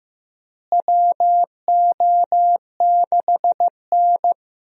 Morse code, heard as a keyed tone, about 16 wpm, 700 Hz. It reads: WO6N
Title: USA Call Signs - 4 Characters - Repeated 2x - 15wpm